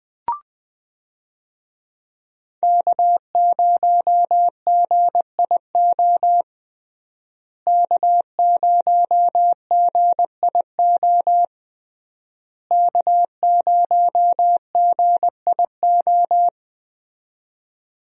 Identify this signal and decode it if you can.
Morse code, heard as a keyed tone, about 21 wpm, 700 Hz. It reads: K0GIO K0GIO K0GIO
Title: USA Call Signs - 5 Characters - Repeated 3x - 20wpm